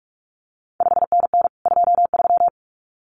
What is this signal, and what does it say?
Morse code, heard as a keyed tone, about 46 wpm, 700 Hz. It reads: 5NN 23
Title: CQ WW DX Contest Exchange Encore - 45wpm